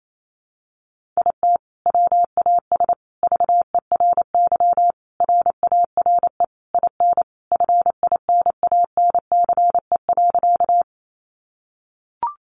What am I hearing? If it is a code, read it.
Morse code, heard as a keyed tone, about 28 wpm, 700 Hz. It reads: IT WAS VERY RARE IN FINANCE.
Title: Sentences from Top 1500 Words 28wpm